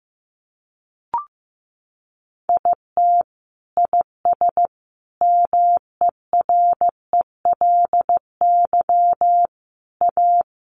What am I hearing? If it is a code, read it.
Morse code, heard as a keyed tone, about 15 wpm, 700 Hz. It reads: IT IS MERELY A
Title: Sentences from Top 1400 Words 15wpm